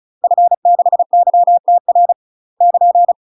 Morse code, heard as a keyed tone, about 35 wpm, 700 Hz. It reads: F6YTR <KN>
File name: UK MASTER QSOs TRAINING / 35WPM / UK MASTER QSOs 0002